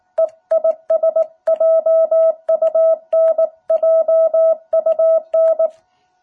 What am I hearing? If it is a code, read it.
Morse code, heard as a keyed tone, about 19 wpm, 660 Hz. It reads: E IS JUN JUN
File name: jh8cbh / 2021cw